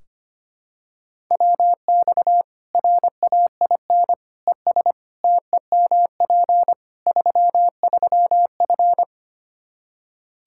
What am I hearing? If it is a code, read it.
Morse code, heard as a keyed tone, about 25 wpm, 700 Hz. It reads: WX RAIN ES TEMP 33F
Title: Practice QSO - Exchange 2 - Info - 25wpm